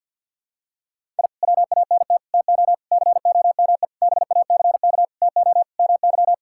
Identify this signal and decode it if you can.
Morse code, heard as a keyed tone, about 50 wpm, 700 Hz. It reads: I WANT TO COME BACK TO MY
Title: Sentences from Top 1000 Words - Review Part 6 - 50wpm